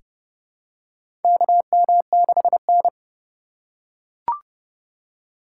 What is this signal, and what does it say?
Morse code, heard as a keyed tone, about 29 wpm, 700 Hz. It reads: KM6N
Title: USA Call Signs - 4 Characters - 2x Character Spacing - 30wpm